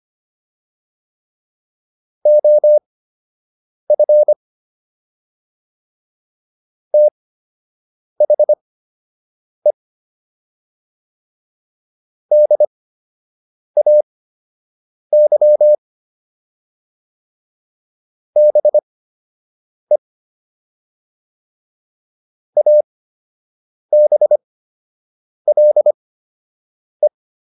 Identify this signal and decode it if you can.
Morse code, heard as a keyed tone, about 25 wpm, spaced as for 7 wpm, 600 Hz. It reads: OF THE DAY BE ABLE